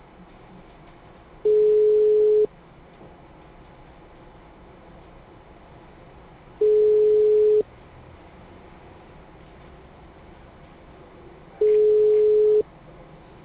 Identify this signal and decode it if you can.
Ringback tone (Europe, 425 Hz).
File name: out- -2014- - - .691